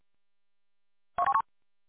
There is no code.